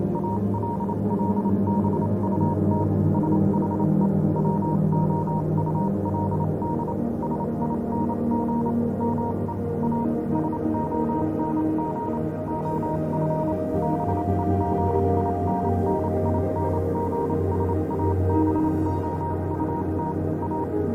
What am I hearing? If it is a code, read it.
Morse code, heard as a keyed tone, about 28 wpm, spaced as for 25 wpm, 940 Hz, with noise. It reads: ALL BUT SHE WOULD SING ME A SONG OF BEGINNINGS ABOUT A
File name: Vylet Pony / aura / do you remember the song she sang (morse code version)